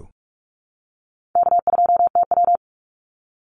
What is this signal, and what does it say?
Morse code, heard as a keyed tone, about 44 wpm, 700 Hz. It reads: K2TW
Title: Call Signs - 45wpm